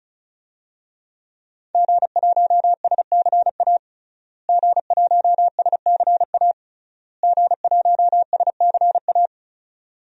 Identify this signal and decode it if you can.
Morse code, heard as a keyed tone, about 34 wpm, 700 Hz. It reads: G1SCA G1SCA G1SCA
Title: DXCC-100 Call Signs - 5 Characters - Repeated 3x - 35wpm